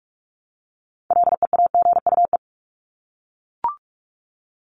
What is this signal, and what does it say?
Morse code, heard as a keyed tone, about 44 wpm, 700 Hz. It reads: LEAGUE
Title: Top 1751-2000 Words 45wpm